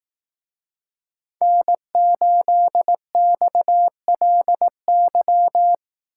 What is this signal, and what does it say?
Morse code, heard as a keyed tone, about 18 wpm, 700 Hz. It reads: N8XLY
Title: Speed Racing - USA 1x3 Call Signs 15wpm